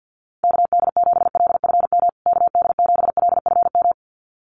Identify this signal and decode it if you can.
Morse code, heard as a keyed tone, about 50 wpm, 700 Hz. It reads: KD7LFM KD7LFM
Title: USA Call Signs Repeated 2x 50wpm